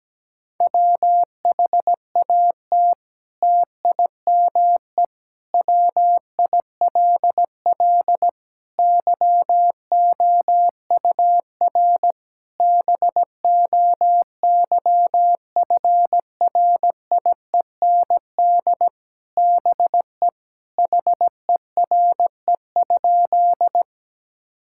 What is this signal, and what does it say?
Morse code, heard as a keyed tone, about 17 wpm, 700 Hz. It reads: WHAT TIME WILL YOUR BOYFRIEND BE HERE?
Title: Sentences from Top 1000 Words - Review Part 5 - 17wpm